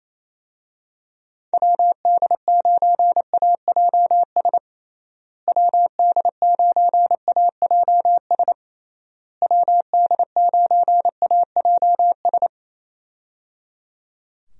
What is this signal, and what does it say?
Morse code, heard as a keyed tone, about 28 wpm, 700 Hz. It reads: WD9AJS WD9AJS WD9AJS
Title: USA Call Signs - 6 Characters - Repeated 3x - 28wpm